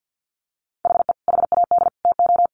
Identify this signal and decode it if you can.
Morse code, heard as a keyed tone, about 49 wpm, 700 Hz. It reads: HE HAD TO
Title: Sentences from Top 1100 Words 50wpm